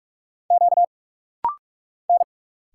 Morse code, heard as a keyed tone, about 46 wpm, 700 Hz. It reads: Q N
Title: Single Letter-Number - Rapid-Fire 45wpm